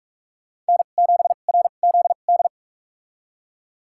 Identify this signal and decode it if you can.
Morse code, heard as a keyed tone, about 46 wpm, spaced as for 33 wpm, 700 Hz. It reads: N7RZD
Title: USA Call Signs - 5 Characters - 2x Character Spacing - 45wpm